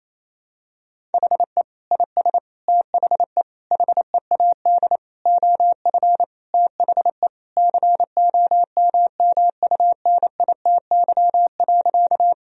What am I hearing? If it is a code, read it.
Morse code, heard as a keyed tone, about 28 wpm, 700 Hz. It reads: HE IS THE HEAD OF THE COMMUNITY.
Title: Sentences from Top 1000 Words - Review Part 4 - 28wpm